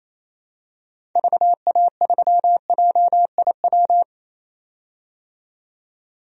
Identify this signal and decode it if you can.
Morse code, heard as a keyed tone, about 28 wpm, 700 Hz. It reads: VA3JIW
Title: DXCC-100 Call Signs - 6 Characters - 28wpm